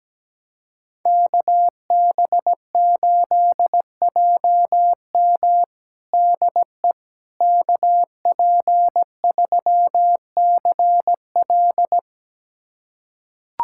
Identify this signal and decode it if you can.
Morse code, heard as a keyed tone, about 17 wpm, 700 Hz. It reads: KB8JM DE KP3CL
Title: USA Callsign DE Callsign - 5 Characters - 17wpm